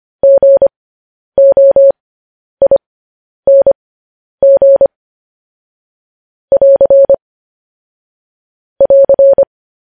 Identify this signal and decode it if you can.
Morse code, heard as a keyed tone, about 25 wpm, spaced as for 10 wpm, 550 Hz. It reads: GOING <AR> <AR>
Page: news headlines in morse code at 10 wpm